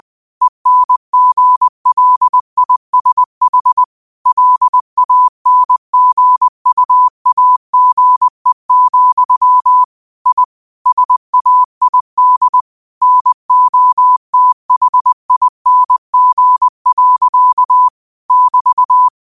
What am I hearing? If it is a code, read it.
Morse code, heard as a keyed tone, about 20 wpm, 1000 Hz. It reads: ENGLISH LANGUAGE, I SAID NOTHING. <BT>
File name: qotdfiles / 20wpm